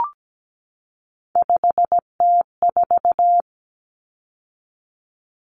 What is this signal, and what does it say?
Morse code, heard as a keyed tone, about 17 wpm, 700 Hz. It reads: 5T4
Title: Common Cut Numbers - 3-Digit 17wpm